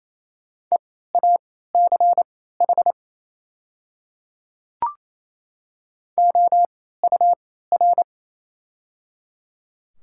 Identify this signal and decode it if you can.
Morse code, heard as a keyed tone, about 28 wpm, spaced as for 16 wpm, 700 Hz. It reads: EACH OUR
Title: 3x Character Spacing - Top 200 Words - 28wpm